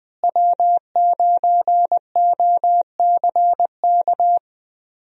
Morse code, heard as a keyed tone, about 20 wpm, 700 Hz. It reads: W9OCK
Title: USA Call Signs - 5 Characters - Repeated 2x - 20wpm